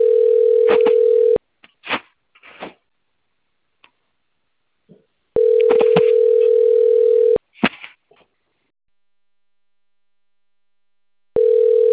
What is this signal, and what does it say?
Ringback tone (North America), 440 + 480 Hz.